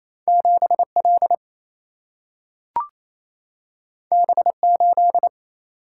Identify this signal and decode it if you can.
Morse code, heard as a keyed tone, about 28 wpm, 700 Hz. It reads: 7L B8
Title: Single-Letter Single-Number 28wpm